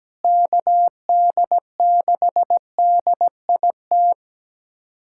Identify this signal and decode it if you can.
Morse code, heard as a keyed tone, about 17 wpm, 700 Hz. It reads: KD6DIT
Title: USA Callsign DE Callsign - 6 Characters - 17wpm